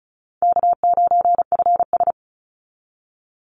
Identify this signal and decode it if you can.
Morse code, heard as a keyed tone, about 34 wpm, 700 Hz. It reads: K9FS
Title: Easy Call Signs Repeated 2x - 35wpm